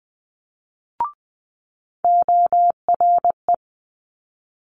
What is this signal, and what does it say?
Morse code, heard as a keyed tone, about 20 wpm, 700 Hz. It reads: ORE
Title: Multi-letter Phonemes Quick Fire 20wpm